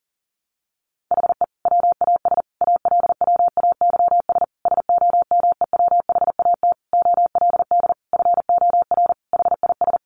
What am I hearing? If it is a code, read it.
Morse code, heard as a keyed tone, about 37 wpm, 700 Hz. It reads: HE WAS ALWAYS SOMEWHAT OLD FOR HIS
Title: Sentences from Top 1300 Words 40wpm